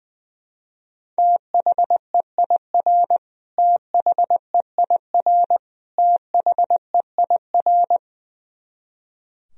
Morse code, heard as a keyed tone, about 20 wpm, 700 Hz. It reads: THEIR THEIR THEIR
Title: Top 100 Words Repeated 20wpm